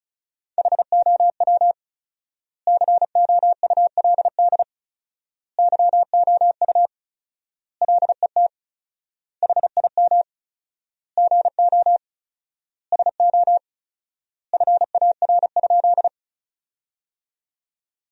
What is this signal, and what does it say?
Morse code, heard as a keyed tone, about 36 wpm, 700 Hz. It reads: HOW COULD YOU LET HIM GO SO FAR?
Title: 4x Word-Spacing - Sentences from Top 300 Words - 35wpm